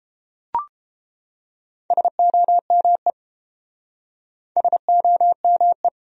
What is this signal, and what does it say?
Morse code, heard as a keyed tone, about 32 wpm, 700 Hz. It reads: SOME SOME